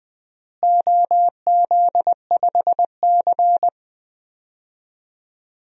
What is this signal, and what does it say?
Morse code, heard as a keyed tone, about 20 wpm, 700 Hz. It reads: OZ5C
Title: DXCC-100 Call Signs - 4 Characters - Repeated 2x - 20wpm